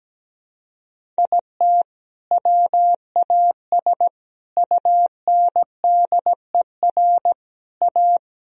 Morse code, heard as a keyed tone, about 17 wpm, 700 Hz. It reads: IT WAS UNDER A
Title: Sentences from Top 1000 Words - Review Part 1 - 17wpm